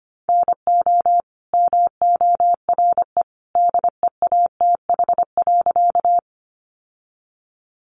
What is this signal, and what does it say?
Morse code, heard as a keyed tone, about 25 wpm, 700 Hz. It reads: NO MORE DEATH.
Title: Sentences from Top 1100 Words 25wpm